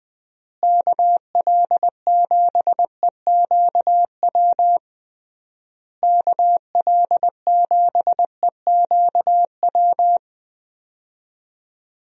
Morse code, heard as a keyed tone, about 20 wpm, 700 Hz. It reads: KL7EQW KL7EQW
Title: DXCC-100 Call Signs - 6 Characters - Repeated 3x - 20wpm